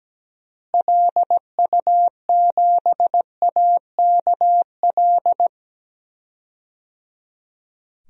Morse code, heard as a keyed tone, about 17 wpm, 700 Hz. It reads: LU7AKL
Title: DXCC-100 Call Signs - 6 Characters - Repeated 3x - 17wpm